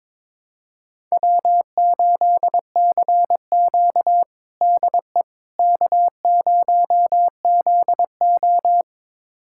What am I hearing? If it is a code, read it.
Morse code, heard as a keyed tone, about 22 wpm, 700 Hz. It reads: W8CQ DE K0ZO
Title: USA Callsign DE Callsign - 4 Characters - 22wpm